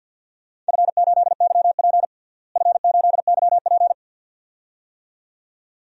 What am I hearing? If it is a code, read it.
Morse code, heard as a keyed tone, about 52 wpm, 700 Hz. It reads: F8YP F8YP
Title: DXCC-100 Call Signs - 4 Characters - Repeated 2x - 50wpm